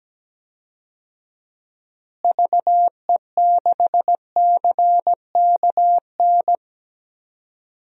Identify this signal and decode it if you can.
Morse code, heard as a keyed tone, about 17 wpm, 700 Hz. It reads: VE6CKN
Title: DXCC-100 Call Signs - 6 Characters - Repeated 3x - 17wpm